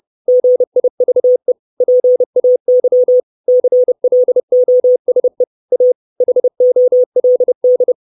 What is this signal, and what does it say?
Morse code, heard as a keyed tone, about 30 wpm, 500 Hz. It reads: GIVE PAY CLOSE A HOLD
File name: 632 Words 30WPM